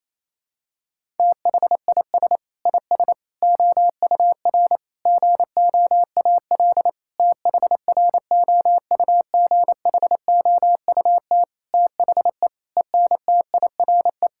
Morse code, heard as a keyed tone, about 28 wpm, 700 Hz. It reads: THIS IS OUR GOAL THROUGHOUT THE ENTIRE
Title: Sentences from Top 1000 Words - Review Part 5 - 28wpm